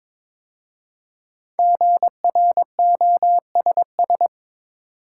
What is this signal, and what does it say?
Morse code, heard as a keyed tone, about 22 wpm, 700 Hz. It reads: GROSS
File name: 5 Letter Words - First 3 Letters Given - 22wpm